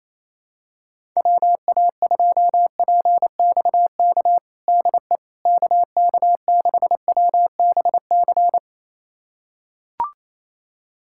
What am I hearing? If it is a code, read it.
Morse code, heard as a keyed tone, about 28 wpm, 700 Hz. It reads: WA2PXK DE KK6WBC
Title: USA Callsign DE Callsign - 6 Characters - 28wpm